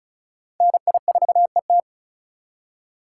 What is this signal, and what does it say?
Morse code, heard as a keyed tone, about 45 wpm, 700 Hz. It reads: NI4ET